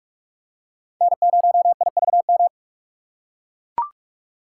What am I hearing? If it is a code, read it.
Morse code, heard as a keyed tone, about 41 wpm, 700 Hz. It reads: N0IVM